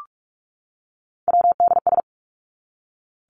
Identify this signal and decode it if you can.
Morse code, heard as a keyed tone, about 46 wpm, 700 Hz. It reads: WDS
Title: Abbreviations 45wpm